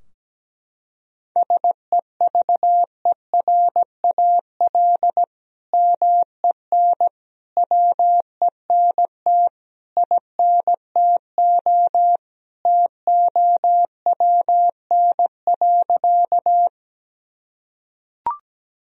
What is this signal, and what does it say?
Morse code, heard as a keyed tone, about 17 wpm, 700 Hz. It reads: SEVERAL MEN WENT INTO TOWN.
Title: Sentences from Top 1000 Words - Review Part 2 - 17wpm